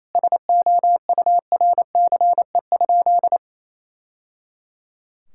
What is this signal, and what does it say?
Morse code, heard as a keyed tone, about 27 wpm, 700 Hz. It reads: SOURCE?
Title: Sentences from Top 1000 Words - Review Part 5 - 28wpm